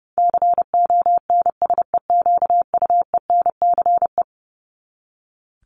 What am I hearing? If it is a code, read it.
Morse code, heard as a keyed tone, about 30 wpm, 700 Hz. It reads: CONSEQUENCE
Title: LY Suffix - Top 100 Words with Base Words 30wpm